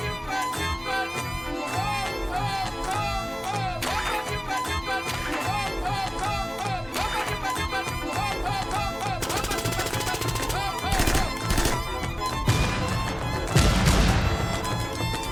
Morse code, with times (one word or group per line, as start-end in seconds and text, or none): none